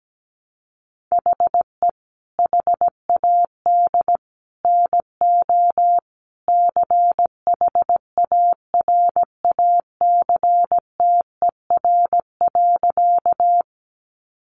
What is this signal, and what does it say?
Morse code, heard as a keyed tone, about 17 wpm, 700 Hz. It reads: HE HAD NO CHARACTER.
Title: Sentences from Top 1200 Words 17wpm